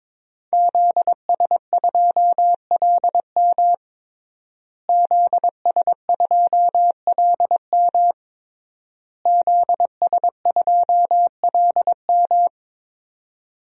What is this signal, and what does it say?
Morse code, heard as a keyed tone, about 22 wpm, 700 Hz. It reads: ZS2LM ZS2LM ZS2LM
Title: DXCC-100 Call Signs - 5 Characters - Repeated 3x - 22wpm